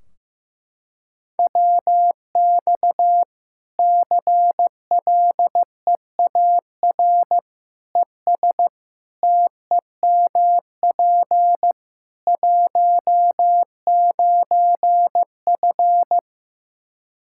Morse code, heard as a keyed tone, about 15 wpm, 700 Hz. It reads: WX CLEAR ES TEMP 19F
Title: Practice QSO - Exchange 2 - Info - 15wpm